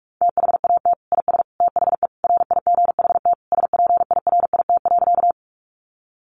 Morse code, heard as a keyed tone, about 44 wpm, 700 Hz. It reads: THAT IS THE RIGHT SPIRIT.
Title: Sentences from Top 1400 Words 45wpm